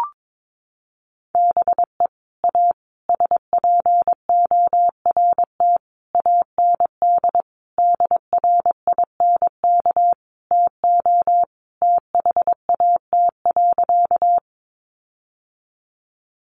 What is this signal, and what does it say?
Morse code, heard as a keyed tone, about 22 wpm, 700 Hz. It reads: BE A SPORT AND DRINK TO THAT.
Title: Sentences from Top 1500 Words 22wpm